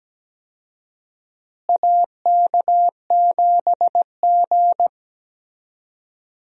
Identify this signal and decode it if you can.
Morse code, heard as a keyed tone, about 17 wpm, 700 Hz. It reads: AK7G
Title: USA Call Signs - 4 Characters - Repeated 2x - 17wpm